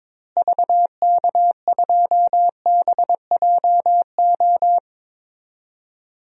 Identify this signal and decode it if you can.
Morse code, heard as a keyed tone, about 21 wpm, 700 Hz. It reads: VK2BJO
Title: DXCC-100 Call Signs - 6 Characters - Repeated 2x - 22wpm